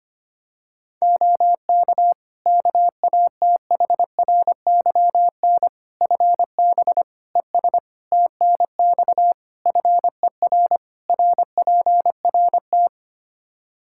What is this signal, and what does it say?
Morse code, heard as a keyed tone, about 25 wpm, 700 Hz. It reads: OK KATHRYN FB ES TNX FER RPRT
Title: K1USN SST QSO Practice 25wpm